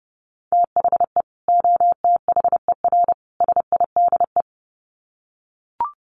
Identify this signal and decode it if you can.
Morse code, heard as a keyed tone, about 30 wpm, 700 Hz. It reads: THE OTHER SIDE
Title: Sets of 3 Words 30wpm